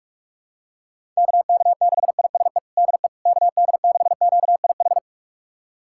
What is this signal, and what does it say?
Morse code, heard as a keyed tone, about 46 wpm, 700 Hz. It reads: KK6ISE DE KD6QIH
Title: USA Callsign DE Callsign - 6 Characters - 45wpm